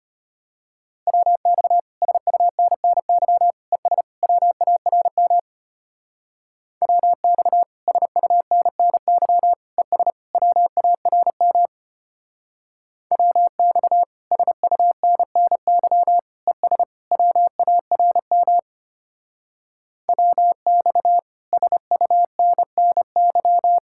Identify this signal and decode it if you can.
Morse code, heard as a keyed tone, about 31 wpm, 700 Hz. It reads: WX SUNNY ES WARM WX SUNNY ES WARM WX SUNNY ES WARM WX SUNNY
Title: Speed-Racing Common QSO Phrases - 25wpm